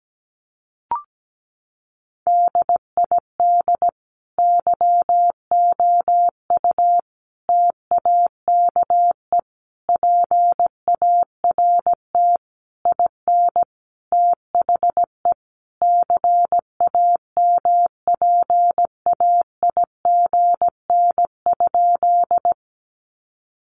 Morse code, heard as a keyed tone, about 17 wpm, 700 Hz. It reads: DID YOU TAKE PART IN THE CAMPAIGN?
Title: Sentences from Top 1200 Words 17wpm